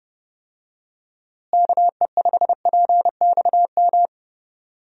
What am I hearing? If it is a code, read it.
Morse code, heard as a keyed tone, about 30 wpm, 700 Hz. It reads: KE5PXM
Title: USA Call Signs - 6 Characters - Repeated 3x - 30wpm